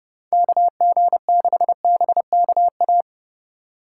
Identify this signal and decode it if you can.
Morse code, heard as a keyed tone, about 29 wpm, 700 Hz. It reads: KG6BKA